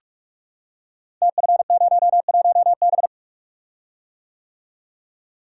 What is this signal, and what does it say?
Morse code, heard as a keyed tone, about 44 wpm, 690 Hz. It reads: TF01B
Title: Code Groups - 5 Letter:Numbers - 45wpm